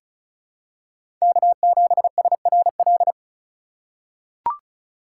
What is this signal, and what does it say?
Morse code, heard as a keyed tone, about 34 wpm, 700 Hz. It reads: K7SRL
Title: USA Call Signs - 5 Characters - 4x Character Spacing - 35wpm